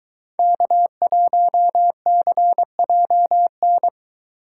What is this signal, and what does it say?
Morse code, heard as a keyed tone, about 22 wpm, 700 Hz. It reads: K1CJN